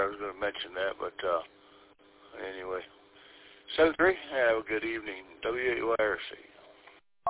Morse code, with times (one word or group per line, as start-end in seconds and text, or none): none